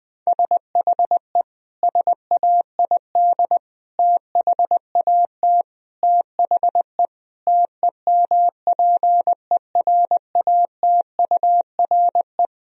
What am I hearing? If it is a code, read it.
Morse code, heard as a keyed tone, about 20 wpm, 700 Hz. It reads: SHE SAID THAT THE TEMPERATURE
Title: Sentences from Top 1000 Words - Review Part 4 - 20wpm